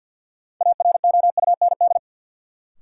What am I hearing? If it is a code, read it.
Morse code, heard as a keyed tone, about 53 wpm, 700 Hz. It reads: AROUND
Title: Sets of 2 Words - New Word Drill - 50wpm